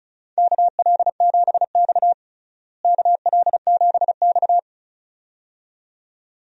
Morse code, heard as a keyed tone, about 34 wpm, 700 Hz. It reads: KL7X KL7X
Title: DXCC-100 Call Signs - 4 Characters - Repeated 2x - 35wpm